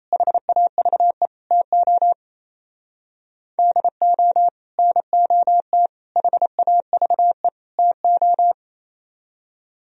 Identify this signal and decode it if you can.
Morse code, heard as a keyed tone, about 37 wpm, 700 Hz. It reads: HAVE TO DO NOT HAVE TO